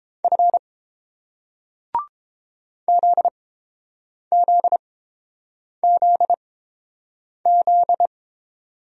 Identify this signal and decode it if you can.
Morse code, heard as a keyed tone, about 31 wpm, spaced as for 7 wpm, 690 Hz. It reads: F ZZZZ